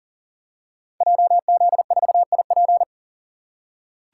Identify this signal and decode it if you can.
Morse code, heard as a keyed tone, about 38 wpm, 700 Hz. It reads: JZ4IP
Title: Code Groups - 5 Letter:Numbers - 40wpm